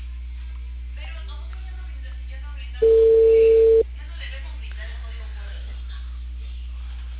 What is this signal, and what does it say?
Ringback tone (China, 450 Hz), with speech.